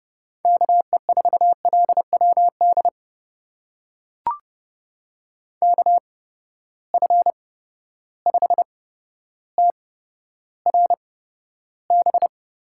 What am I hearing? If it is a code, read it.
Morse code, heard as a keyed tone, about 30 wpm, spaced as for 13 wpm, 700 Hz. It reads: KE4LWD KF5TRB
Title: USA Call Signs - 6 Characters - 8x Character Spacing - 30wpm